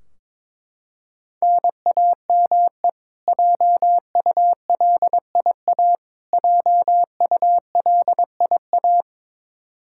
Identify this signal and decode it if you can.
Morse code, heard as a keyed tone, about 21 wpm, 700 Hz. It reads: NAME JULIA JULIA
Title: Practice QSO - Exchange 1 - Beginning - 22wpm